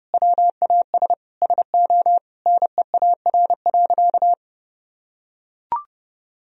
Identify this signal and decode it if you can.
Morse code, heard as a keyed tone, about 30 wpm, 700 Hz. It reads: WAS SO NEAR.
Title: Sentences from Top 300 Words and Abbreviations 30wpm